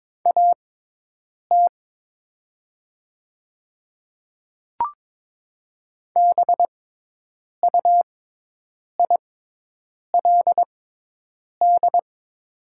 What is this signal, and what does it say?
Morse code, heard as a keyed tone, about 22 wpm, spaced as for 8 wpm, 700 Hz. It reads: AT BUILD
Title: 6x Character Spacing - Top 300 Words - 22wpm